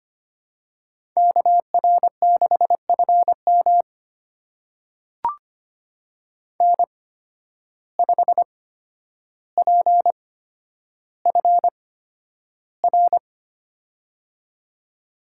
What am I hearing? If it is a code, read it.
Morse code, heard as a keyed tone, about 25 wpm, spaced as for 11 wpm, 700 Hz. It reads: KR6FM N5PFR
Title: USA Call Signs - 5 Characters - 8x Character Spacing - 25wpm